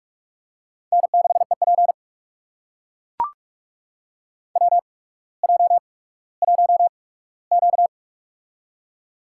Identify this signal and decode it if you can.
Morse code, heard as a keyed tone, about 44 wpm, spaced as for 19 wpm, 700 Hz. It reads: N6EP WJ1Q
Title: USA Call Signs - 8x Character Spacing - 45wpm